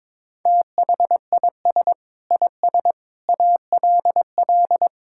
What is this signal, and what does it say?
Morse code, heard as a keyed tone, about 22 wpm, 700 Hz. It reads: THIS IS ALL
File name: practice / 22wpm / Build A Phrase - 5-Word Phrases - 22wpm